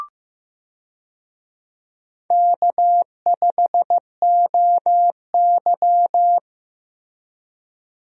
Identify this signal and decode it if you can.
Morse code, heard as a keyed tone, about 15 wpm, 700 Hz. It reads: K5OY
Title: USA Call Signs - 4 Characters - Repeated 2x - 15wpm